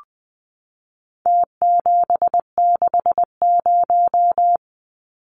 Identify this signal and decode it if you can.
Morse code, heard as a keyed tone, about 20 wpm, 700 Hz. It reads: T760
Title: Numbers - Contest Style - 4-Digit 20wpm